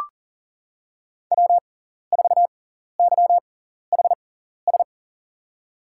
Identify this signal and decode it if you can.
Morse code, heard as a keyed tone, about 39 wpm, spaced as for 14 wpm, 700 Hz. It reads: W4YHS
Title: USA Call Signs - 6x Character Spacing - 40wpm